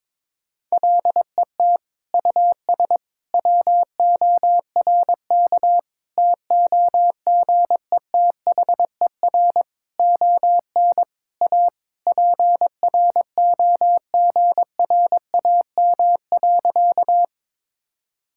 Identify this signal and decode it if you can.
Morse code, heard as a keyed tone, about 22 wpm, 700 Hz. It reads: LET US WORK TOGETHER ON A PROGRAM.
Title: Sentences from Top 1000 Words - Review Part 5 - 22wpm